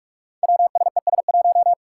Morse code, heard as a keyed tone, about 46 wpm, 700 Hz. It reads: WSES1
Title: Five Challenging Letter-Number Combinations 45wpm